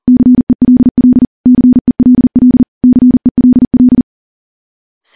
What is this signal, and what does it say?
Morse code, heard as a keyed tone, about 40 wpm, 250 Hz. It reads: CELL CELL CELL
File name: morse / morse-training-40WPM